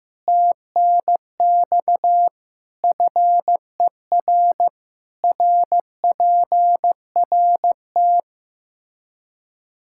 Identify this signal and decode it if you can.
Morse code, heard as a keyed tone, about 15 wpm, 700 Hz. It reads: TNX FER RPRT
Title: Practice QSO - Exchange 1 - Beginning - 15wpm